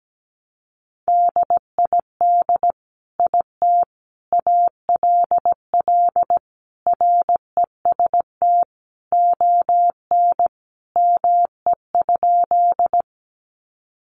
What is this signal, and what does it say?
Morse code, heard as a keyed tone, about 17 wpm, 700 Hz. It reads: DID IT ALL REST ON ME?
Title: Sentences from Top 500 Words 17wpm